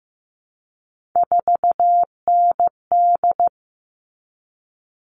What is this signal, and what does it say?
Morse code, heard as a keyed tone, about 15 wpm, 700 Hz. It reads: 4ND
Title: All Cut Numbers - 3-Digit 15wpm